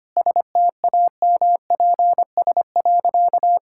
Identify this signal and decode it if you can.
Morse code, heard as a keyed tone, about 25 wpm, 690 Hz. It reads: STAMPS.
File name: Easy Sentences with Hints - 25wpm